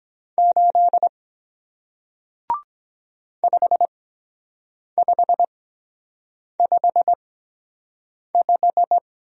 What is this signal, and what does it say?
Morse code, heard as a keyed tone, about 23 wpm, spaced as for 7 wpm, 700 Hz. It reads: 8 5555